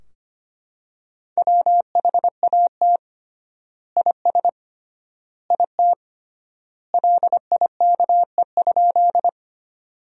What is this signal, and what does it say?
Morse code, heard as a keyed tone, about 25 wpm, 700 Hz. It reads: WHAT IS IT LIKE?
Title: 3x Word-Spacing - Sentences from Top 100 Words - 25wpm